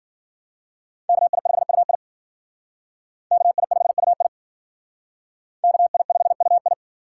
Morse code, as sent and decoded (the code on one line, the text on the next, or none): -.- .. ..... ..- .. / -.- .. ..... ..- .. / -.- .. ..... ..- ..
KI5UI KI5UI KI5UI